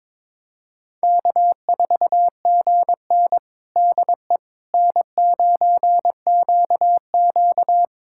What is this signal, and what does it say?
Morse code, heard as a keyed tone, about 22 wpm, 700 Hz. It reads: K4GN DE N9QQ
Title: USA Callsign DE Callsign - 4 Characters - 22wpm